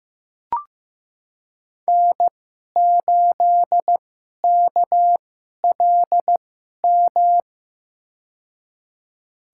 Morse code, heard as a keyed tone, about 15 wpm, spaced as for 11 wpm, 700 Hz. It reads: N8KLM
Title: USA Call Signs - 5 Characters - 2x Character Spacing - 15wpm